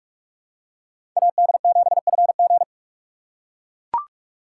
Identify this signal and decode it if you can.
Morse code, heard as a keyed tone, about 42 wpm, 700 Hz. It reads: AD7FG
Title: USA Call Signs - 5 Characters - 3x Character Spacing - 45wpm